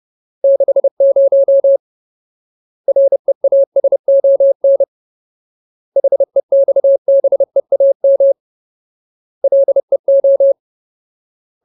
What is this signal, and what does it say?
Morse code, heard as a keyed tone, about 30 wpm, 550 Hz. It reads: 60 REASON HEXBEAM LEO
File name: Ultimate Word List / 30WPM Ultimate